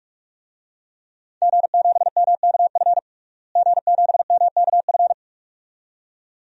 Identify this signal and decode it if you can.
Morse code, heard as a keyed tone, about 47 wpm, 700 Hz. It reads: G7MKF G7MKF
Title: DXCC-100 Call Signs - 5 Characters - Repeated 2x - 45wpm